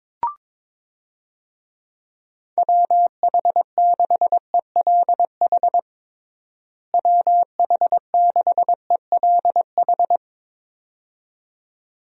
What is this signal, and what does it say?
Morse code, heard as a keyed tone, about 21 wpm, 700 Hz. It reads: WH6ELH WH6ELH
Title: USA Call Signs - 6 Characters - Repeated 2x - 22wpm